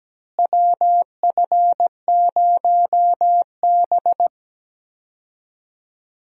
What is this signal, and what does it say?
Morse code, heard as a keyed tone, about 17 wpm, 700 Hz. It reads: WF0B
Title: USA Call Signs - 4 Characters - Repeated 3x - 17wpm